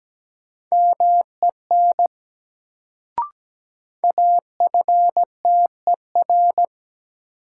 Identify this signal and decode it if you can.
Morse code, heard as a keyed tone, about 17 wpm, 700 Hz. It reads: MEN AFTER
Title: Sentences from Top 200 Words - New Word Drill - 17wpm